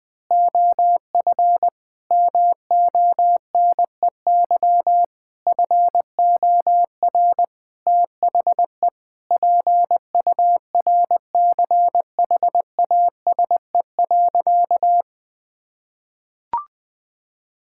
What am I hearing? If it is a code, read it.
Morse code, heard as a keyed tone, about 20 wpm, 700 Hz. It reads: OF MONEY FOR THE PURCHASE.
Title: Sentences from Top 1000 Words - Review Part 5 - 20wpm